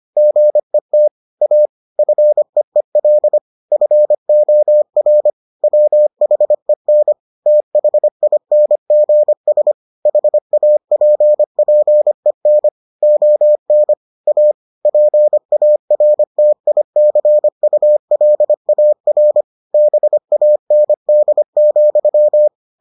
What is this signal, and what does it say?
Morse code, heard as a keyed tone, about 25 wpm, 600 Hz. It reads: GET A FEEL FOR WHEN THINGS HAPPEN ON A PARTICULAR BAND,